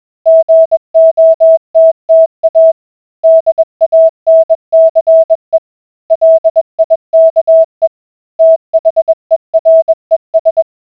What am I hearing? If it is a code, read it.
Morse code, heard as a keyed tone, about 21 wpm, 650 Hz. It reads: GOTTA DANCE LIKE THERES